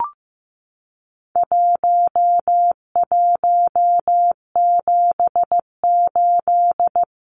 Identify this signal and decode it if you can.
Morse code, heard as a keyed tone, about 15 wpm, 700 Hz. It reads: 1178
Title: Numbers - 4-Digit 15wpm